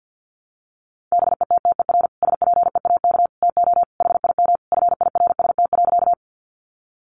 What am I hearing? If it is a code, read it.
Morse code, heard as a keyed tone, about 50 wpm, 700 Hz. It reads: BETTER SPEAK TO HIM FIRST.
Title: Sentences from Top 1100 Words 50wpm